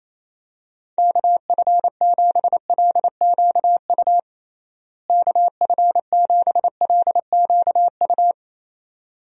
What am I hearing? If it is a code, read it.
Morse code, heard as a keyed tone, about 28 wpm, 700 Hz. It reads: KF7LQU KF7LQU
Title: USA Call Signs - 6 Characters - Repeated 2x - 28wpm